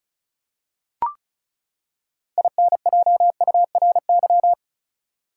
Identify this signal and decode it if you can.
Morse code, heard as a keyed tone, about 36 wpm, 700 Hz. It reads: INJURY
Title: Six Letter Words - 35wpm